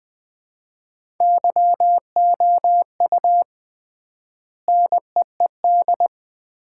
Morse code, heard as a keyed tone, about 20 wpm, 700 Hz. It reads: YOU NEED
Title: 3x Word-Spacing - Sets of 3 Words - 20wpm